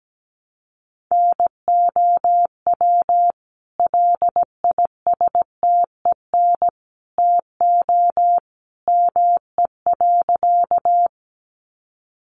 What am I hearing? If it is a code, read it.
Morse code, heard as a keyed tone, about 17 wpm, 700 Hz. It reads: NOW LISTEN TO ME.
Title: Sentences from Top 400 Words 17wpm